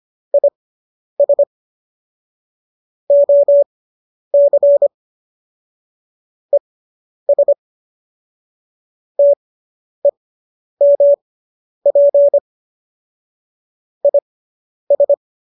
Morse code, heard as a keyed tone, about 25 wpm, spaced as for 10 wpm, 570 Hz. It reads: IS OC ES TEMP IS